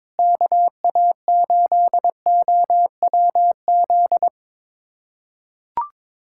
Morse code, heard as a keyed tone, about 22 wpm, 700 Hz. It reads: KA8OWZ
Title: USA Callsign DE Callsign - 6 Characters - 22wpm